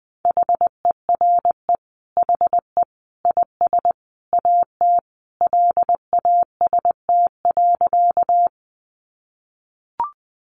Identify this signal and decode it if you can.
Morse code, heard as a keyed tone, about 20 wpm, 700 Hz. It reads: HERE HE IS AT LAST.
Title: Sentences from Top 300 Words 20wpm